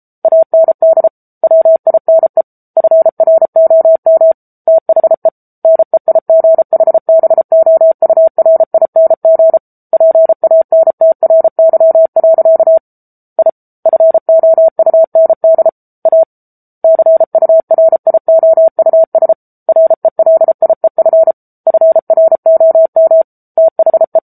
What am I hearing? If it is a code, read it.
Morse code, heard as a keyed tone, about 34 wpm, 650 Hz. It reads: AND WINE FROM THE NEIGHBOURING PANTRY. I FOUND A CURIOUS RELIEF FROM THE
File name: Book Two / 07-Chapter 7 - 33 WPM